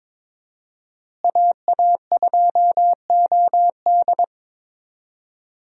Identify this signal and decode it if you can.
Morse code, heard as a keyed tone, about 22 wpm, 700 Hz. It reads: AA2OD
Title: USA Call Signs - 5 Characters - Repeated 3x - 22wpm